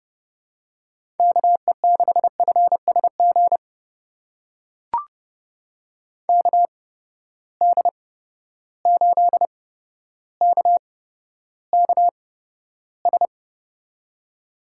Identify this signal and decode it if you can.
Morse code, heard as a keyed tone, about 30 wpm, spaced as for 13 wpm, 700 Hz. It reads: KE6FSG KD8KKS
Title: USA Call Signs - 8x Character Spacing - 30wpm